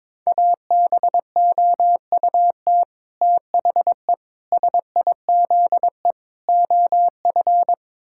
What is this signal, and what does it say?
Morse code, heard as a keyed tone, about 21 wpm, 700 Hz. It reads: ABOUT THE SIZE OF
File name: practice / 22wpm / Build A Phrase - 5-Word Phrases - 22wpm